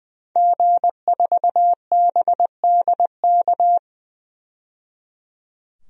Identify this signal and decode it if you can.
Morse code, heard as a keyed tone, about 21 wpm, 700 Hz. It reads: G4BDK
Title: DXCC-100 Call Signs - 5 Characters - 20wpm